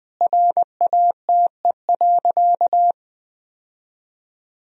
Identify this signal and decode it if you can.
Morse code, heard as a keyed tone, about 20 wpm, 700 Hz. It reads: RATE.
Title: Sentences from Top 1000 Words - Review Part 4 - 20wpm